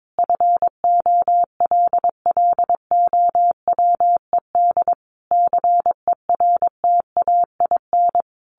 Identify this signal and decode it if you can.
Morse code, heard as a keyed tone, about 22 wpm, 700 Hz. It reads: FOLLOWED CERTAIN
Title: A Sound Of Thunder 22wpm